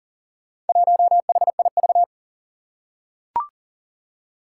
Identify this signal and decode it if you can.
Morse code, heard as a keyed tone, about 38 wpm, 700 Hz. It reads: 1HIV
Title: Four Challenging Letter-Number Combinations 40wpm